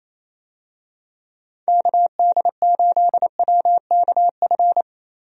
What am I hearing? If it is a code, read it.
Morse code, heard as a keyed tone, about 28 wpm, 700 Hz. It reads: KD8WKF
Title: USA Call Signs - 6 Characters - Repeated 3x - 28wpm